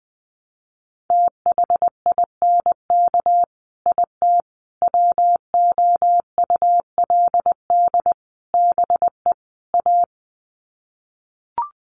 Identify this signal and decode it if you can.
Morse code, heard as a keyed tone, about 20 wpm, 700 Hz. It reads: THINK IT WOULD BE A
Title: Sets of 5 Words Encore2 20wpm